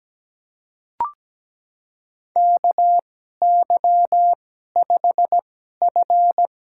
Morse code, heard as a keyed tone, about 17 wpm, spaced as for 12 wpm, 700 Hz. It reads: KY5F
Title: USA Call Signs - 4 Characters - 2x Character Spacing - 17wpm